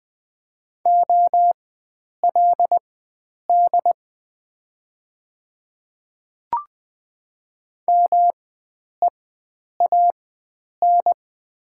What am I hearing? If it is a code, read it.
Morse code, heard as a keyed tone, about 20 wpm, spaced as for 9 wpm, 700 Hz. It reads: OLD MEAN
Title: 4x Character Spacing - Top 300 Words - 20wpm